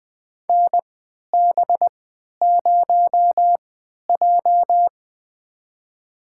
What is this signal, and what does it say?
Morse code, heard as a keyed tone, about 20 wpm, spaced as for 11 wpm, 700 Hz. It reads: NB0J
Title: USA Call Signs - 4 Characters - 3x Character Spacing - 20wpm